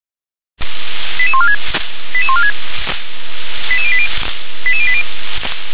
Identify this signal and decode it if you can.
Five-tone selective call (ZVEI), heard as quick runs of five tones, tones of about 70 ms. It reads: 88146 88146 88080 88080